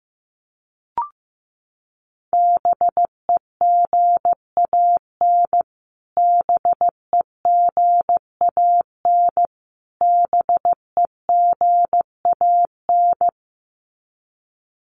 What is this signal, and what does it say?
Morse code, heard as a keyed tone, about 15 wpm, 700 Hz. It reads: BEGAN BEGAN BEGAN
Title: Top 500 Words Repeated 15wpm